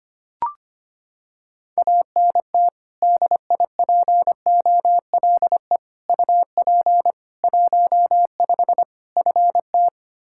Morse code, heard as a keyed tone, about 25 wpm, 700 Hz. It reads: ANT DIPOLE UP 15 FT
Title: Practice QSO - Exchange 2 - Info - 25wpm